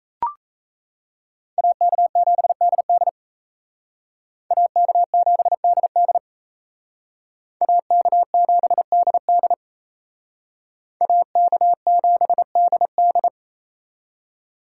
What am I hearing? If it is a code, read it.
Morse code, heard as a keyed tone, about 35 wpm, 700 Hz. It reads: AK7DD AK7DD AK7DD AK7DD